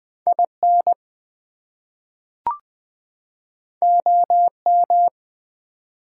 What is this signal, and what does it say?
Morse code, heard as a keyed tone, about 20 wpm, 700 Hz. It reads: IN OM
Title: Top 180 QSO Elements 20wpm